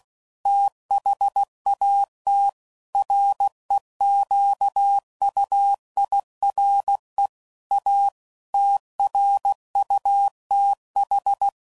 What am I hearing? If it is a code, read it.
Morse code, heard as a keyed tone, about 16 wpm, 800 Hz. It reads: THAT REQUIRE A TRUTH
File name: CW Code Groups @15WPM / Technical Subjects / AND Gate